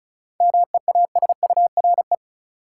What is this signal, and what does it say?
Morse code, heard as a keyed tone, about 36 wpm, 700 Hz. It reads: MEASURE
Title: Sentences from Top 1000 Words - Review Part 1 - New Word Drill - 35wpm